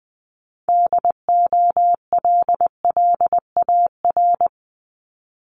Morse code, heard as a keyed tone, about 19 wpm, 690 Hz. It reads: DOLLAR
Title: Multi-letter Phonemes 20wpm